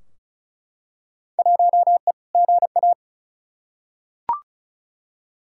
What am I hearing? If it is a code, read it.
Morse code, heard as a keyed tone, about 36 wpm, 700 Hz. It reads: 1E GA
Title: Field Day Exchanges - 35wpm